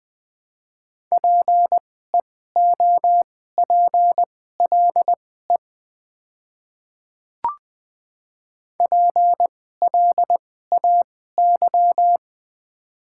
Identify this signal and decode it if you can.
Morse code, heard as a keyed tone, about 20 wpm, spaced as for 14 wpm, 700 Hz. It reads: PEOPLE PLAY
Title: 2x Character Spacing - Top 500 Words - 20wpm